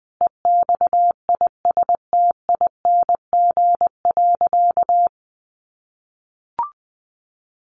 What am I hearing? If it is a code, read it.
Morse code, heard as a keyed tone, about 20 wpm, 700 Hz. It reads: EXISTING.
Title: Sentences from Top 1500 Words 20wpm